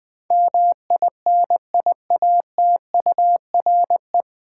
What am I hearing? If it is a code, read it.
Morse code, heard as a keyed tone, about 20 wpm, 700 Hz. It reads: MINIATURE
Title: Difficult Words with Hints - 20wpm